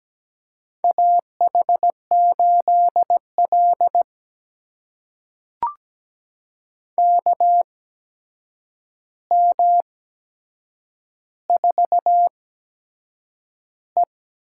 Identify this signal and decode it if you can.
Morse code, heard as a keyed tone, about 17 wpm, spaced as for 7 wpm, 700 Hz. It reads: AH8L KM4E
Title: USA Call Signs - 8x Character Spacing - 17wpm